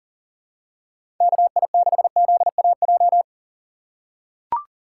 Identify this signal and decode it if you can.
Morse code, heard as a keyed tone, about 40 wpm, 700 Hz. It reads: KI6ZAJ
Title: USA Call Signs - 6 Characters - 3x Character Spacing - 40wpm